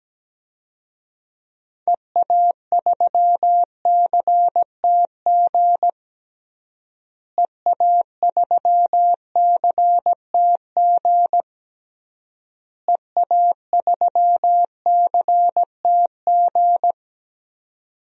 Morse code, heard as a keyed tone, about 17 wpm, 700 Hz. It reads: EA3CTG EA3CTG EA3CTG
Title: DXCC-100 Call Signs - 6 Characters - Repeated 3x - 17wpm